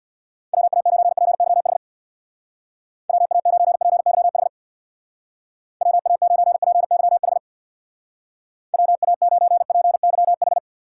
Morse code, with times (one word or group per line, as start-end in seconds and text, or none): 0.5-1.8 WA9PYH
3.1-4.5 WA9PYH
5.8-7.4 WA9PYH
8.7-10.6 WA9PYH